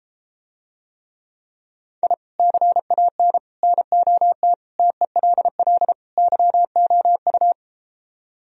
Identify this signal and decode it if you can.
Morse code, heard as a keyed tone, about 33 wpm, 700 Hz. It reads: I CAN NOT TELL YOU